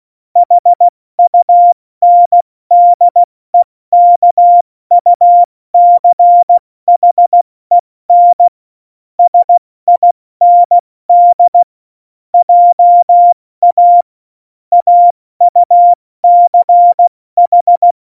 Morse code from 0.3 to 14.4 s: HUNDEKUCHEN SIND JA